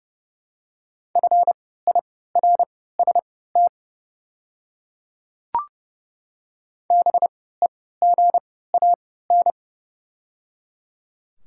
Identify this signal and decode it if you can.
Morse code, heard as a keyed tone, about 30 wpm, spaced as for 17 wpm, 700 Hz. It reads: FIRST BEGAN
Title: 3x Character Spacing - Top 300 Words - 30wpm